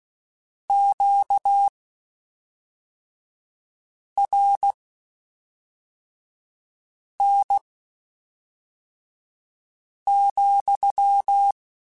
Morse code, heard as a keyed tone, about 16 wpm, spaced as for 3 wpm, 800 Hz. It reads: QRN,
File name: Q Calls / Q Calls-mp3s / Q Calls SPSL2 @5